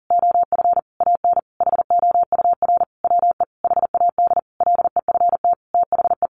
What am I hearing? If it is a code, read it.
Morse code, heard as a keyed tone, about 40 wpm, 700 Hz. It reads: OF AN HOUR WE HAD LEFT THE
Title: Sentences from Top 1300 Words 40wpm